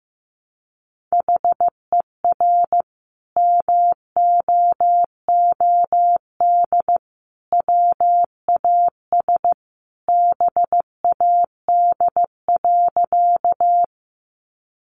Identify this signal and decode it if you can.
Morse code, heard as a keyed tone, about 14 wpm, 700 Hz. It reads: HER MOOD WAS BAD.
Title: Sentences from Top 1100 Words 15wpm